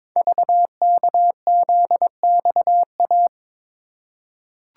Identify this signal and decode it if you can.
Morse code, heard as a keyed tone, about 22 wpm, 700 Hz. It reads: VKZXA
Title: Code Groups - 5 Letters - 22wpm